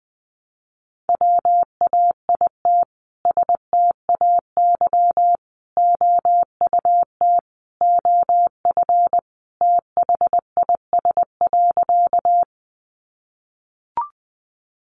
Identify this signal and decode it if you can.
Morse code, heard as a keyed tone, about 20 wpm, 700 Hz. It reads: WAIT STAY OUT OF THIS.
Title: A Sound Of Thunder 20wpm